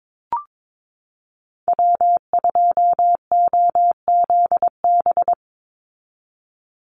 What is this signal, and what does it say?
Morse code, heard as a keyed tone, about 22 wpm, 700 Hz. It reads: W2OZB
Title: Call Signs Repeated 2x - 22wpm